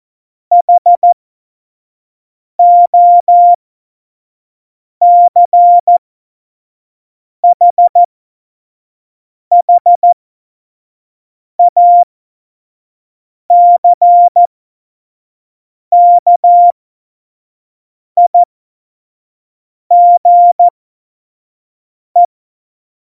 Morse code, heard as a keyed tone, about 14 wpm, spaced as for 5 wpm, 700 Hz. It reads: HOCHHACKIGE
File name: Morsekurs / Witze / Witz 22. Mai 2018, 25 F70